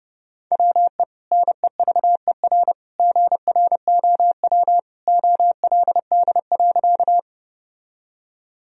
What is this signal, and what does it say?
Morse code, heard as a keyed tone, about 30 wpm, 700 Hz. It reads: WE NEVER GROW OLD.
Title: Sentences from Top 1000 Words - Review Part 1 - 30wpm